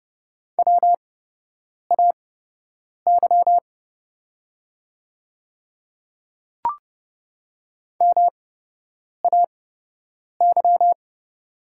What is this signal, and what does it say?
Morse code, heard as a keyed tone, about 29 wpm, spaced as for 8 wpm, 700 Hz. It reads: WAY MAY
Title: 8x Character Spacing - Top 100 Words - 30wpm